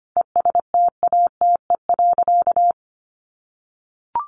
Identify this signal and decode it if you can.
Morse code, heard as a keyed tone, about 25 wpm, 700 Hz. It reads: ESTATE.
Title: Sentences from Top 1200 Words 25wpm